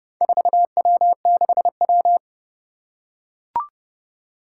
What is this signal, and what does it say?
Morse code, heard as a keyed tone, about 30 wpm, 700 Hz. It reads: 4W6W
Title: Four Challenging Letter-Number Combinations 30wpm